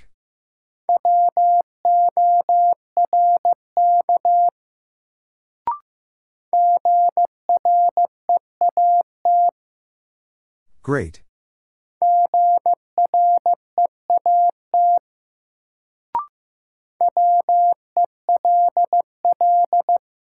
Morse code, heard as a keyed tone, about 15 wpm, 700 Hz. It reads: WORK GREAT GREAT WELL
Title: Sentences from Top 200 Words - New Word Drill - 15wpm